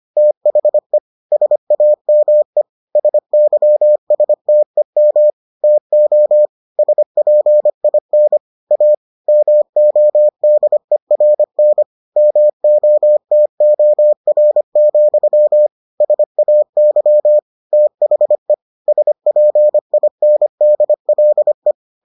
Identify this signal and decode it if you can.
Morse code, heard as a keyed tone, about 25 wpm, 600 Hz. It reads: THE SAME SYSTEM TO SPIN A MODERN MOTOR, SAY THE SPINDLE